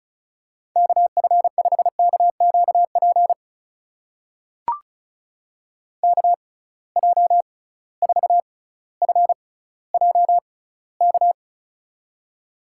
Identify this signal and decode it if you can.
Morse code, heard as a keyed tone, about 35 wpm, spaced as for 18 wpm, 700 Hz. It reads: KF5KQP KJ4FJK
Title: USA Call Signs - 6 Characters - 6x Character Spacing - 35wpm